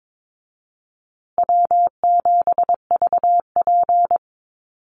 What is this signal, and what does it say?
Morse code, heard as a keyed tone, about 22 wpm, 700 Hz. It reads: W7VP
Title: Easy Call Signs Repeated 3x - 22wpm